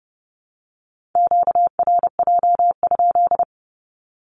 Morse code, heard as a keyed tone, about 30 wpm, 700 Hz. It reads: QRJ?
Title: Q-Codes Repeated 3x 30wpm